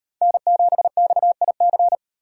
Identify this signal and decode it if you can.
Morse code, heard as a keyed tone, about 38 wpm, 700 Hz. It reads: N7XIC